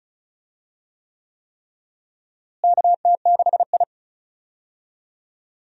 Morse code, heard as a keyed tone, about 36 wpm, 700 Hz. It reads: KT6I
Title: USA Call Signs - 4 Characters - Repeated 3x - 35wpm